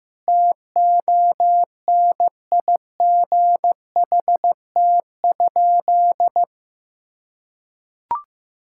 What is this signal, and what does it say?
Morse code, heard as a keyed tone, about 15 wpm, 700 Hz. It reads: TONIGHT?
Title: Sentences from Top 1000 Words - Review Part 5 - 15wpm